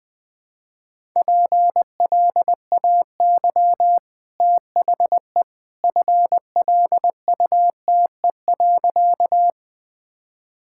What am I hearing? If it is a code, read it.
Morse code, heard as a keyed tone, about 21 wpm, 700 Hz. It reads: PLAY THE FLUTE.
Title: Easy Sentences with Hints - 20wpm